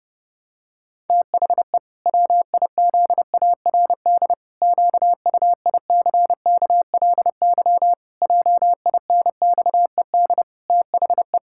Morse code, heard as a keyed tone, about 32 wpm, 700 Hz. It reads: THE WIZARD QUICKLY JINXED THE
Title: Pangram Challenge v2 - 30wpm